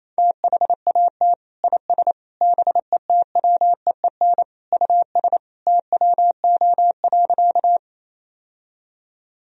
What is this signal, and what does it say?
Morse code, heard as a keyed tone, about 28 wpm, 700 Hz. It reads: THAT IS BETWEEN US TWO.
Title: Sentences from Top 1000 Words - Review Part 1 - 28wpm